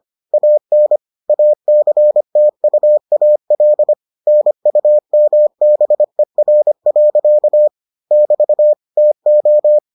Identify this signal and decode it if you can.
Morse code, heard as a keyed tone, about 25 wpm, 600 Hz. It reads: AN ACTUAL NUMBER. <BT> TO